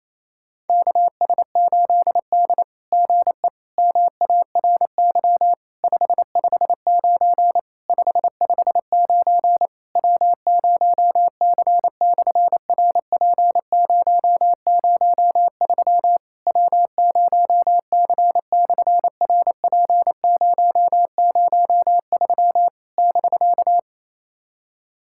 Morse code, heard as a keyed tone, about 27 wpm, 700 Hz. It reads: KS8D GE MARY 559 559 W0C/RP003 W0C/RP003 <BK>
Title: SOTA QSO Practice 28wpm